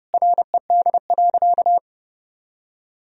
Morse code, heard as a keyed tone, about 30 wpm, 700 Hz. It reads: RED.
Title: Sentences from Top 1000 Words - Review Part 2 - 30wpm